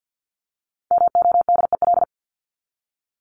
Morse code, heard as a keyed tone, about 50 wpm, 700 Hz. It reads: MODEL
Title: Top 501-750 Words 50wpm